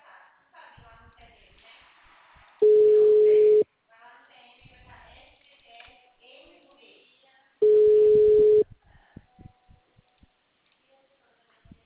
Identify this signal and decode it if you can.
Ringback tone (Europe, 425 Hz), with speech.
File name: out- -2014- - - .72